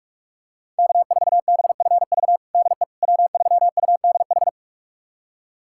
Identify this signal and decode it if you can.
Morse code, heard as a keyed tone, about 44 wpm, 700 Hz. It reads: K4BFV DE W3UDH
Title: USA Callsign DE Callsign - 5 Characters - 45wpm